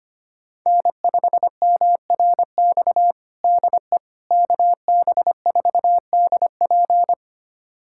Morse code, heard as a keyed tone, about 25 wpm, 700 Hz. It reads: N5MRX DE KB4DP
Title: USA Callsign DE Callsign - 5 Characters - 25wpm